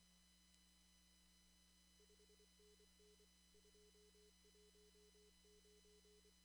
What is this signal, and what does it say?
Morse code, heard as a keyed tone, about 25 wpm, 420 Hz. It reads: 5NN 219
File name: CW / kt5j wpx cw-op-k5pi